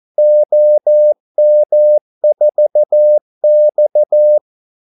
Morse code, heard as a keyed tone, about 14 wpm, 600 Hz. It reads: OM4X
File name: Callsigns-14wpm